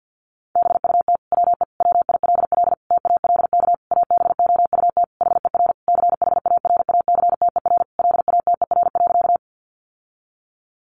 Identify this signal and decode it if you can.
Morse code, heard as a keyed tone, about 50 wpm, 700 Hz. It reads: BUT WE WILL TALK ABOUT HER CHARACTER LATER.
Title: Sentences from Top 1200 Words 50wpm